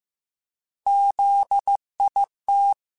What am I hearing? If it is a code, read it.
Morse code, heard as a keyed tone, about 15 wpm, 800 Hz. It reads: ZIT
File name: Alphabetical / Alphabetical mp3s / Zs @13